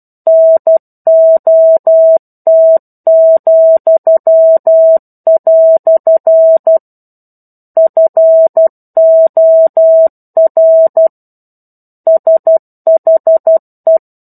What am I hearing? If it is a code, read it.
Morse code, heard as a keyed tone, about 12 wpm, 650 Hz. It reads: NOT," FOR SHE
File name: 01-Chapter-01 12 WPM